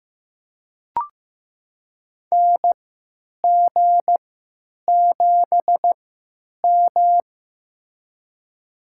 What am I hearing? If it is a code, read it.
Morse code, heard as a keyed tone, about 15 wpm, spaced as for 9 wpm, 700 Hz. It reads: NG7M
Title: USA Call Signs - 4 Characters - 3x Character Spacing - 15wpm